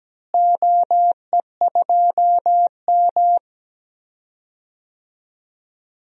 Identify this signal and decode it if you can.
Morse code, heard as a keyed tone, about 17 wpm, 700 Hz. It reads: OE2M
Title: DXCC-100 Call Signs - 4 Characters - Repeated 3x - 17wpm